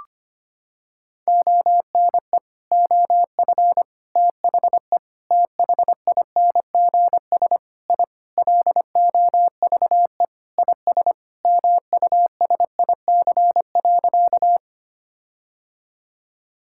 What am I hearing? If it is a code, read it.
Morse code, heard as a keyed tone, about 25 wpm, 700 Hz. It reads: ONE OF THE THINGS I LOVE IS MUSIC.
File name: practice / 25wpm / Sentences with 4-Word Phrases - 25wpm